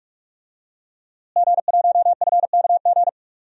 Morse code, heard as a keyed tone, about 43 wpm, 700 Hz. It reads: G1FKG
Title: DXCC-100 Call Signs - 5 Characters - Repeated 2x - 45wpm